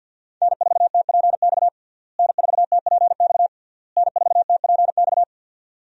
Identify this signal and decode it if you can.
Morse code, heard as a keyed tone, about 52 wpm, 700 Hz. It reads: N4TPX N4TPX N4TPX
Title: USA Call Signs - 5 Characters - Repeated 3x - 50wpm